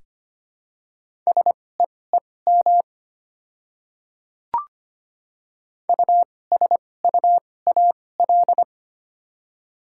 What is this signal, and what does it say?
Morse code, heard as a keyed tone, about 25 wpm, spaced as for 18 wpm, 700 Hz. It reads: SEEM USUAL
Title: 2x Character Spacing - Top 500 Words - 25wpm